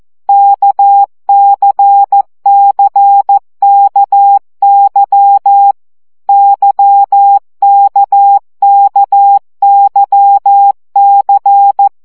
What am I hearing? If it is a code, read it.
Morse code, heard as a keyed tone, about 14 wpm, 800 Hz. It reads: KCCKY YKKYC
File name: CW gyakorlas k c y 12 baud